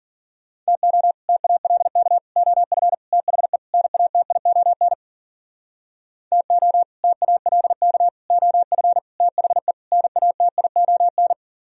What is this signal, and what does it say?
Morse code, heard as a keyed tone, about 51 wpm, 700 Hz. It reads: TO TALK OF THE NATION TO TALK OF THE NATION